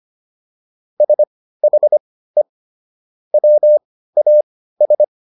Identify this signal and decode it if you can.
Morse code, heard as a keyed tone, about 25 wpm, spaced as for 15 wpm, 600 Hz. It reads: SHE WAS